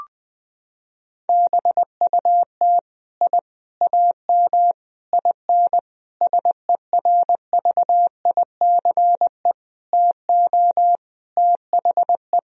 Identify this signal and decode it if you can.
Morse code, heard as a keyed tone, about 20 wpm, 700 Hz. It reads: BUT I AM IN SERVICE TO THE
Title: Sentences from Top 1000 Words - Review Part 4 - 20wpm